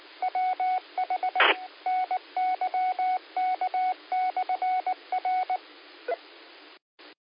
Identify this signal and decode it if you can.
Morse code, heard as a keyed tone, about 19 wpm, 740 Hz. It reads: WH NYK/R
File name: qsorec SimplexLogic 15-38-40